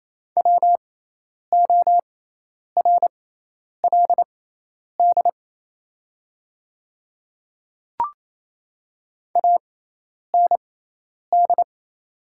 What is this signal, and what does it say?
Morse code, heard as a keyed tone, about 27 wpm, spaced as for 10 wpm, 700 Hz. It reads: WORLD AND
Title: 6x Character Spacing - Top 200 Words - 28wpm